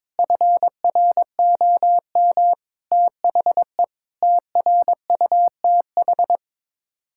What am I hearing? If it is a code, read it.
Morse code, heard as a keyed tone, about 21 wpm, 700 Hz. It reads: FROM THE TRUTH
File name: practice / 22wpm / Sets of 8 Words - 22wpm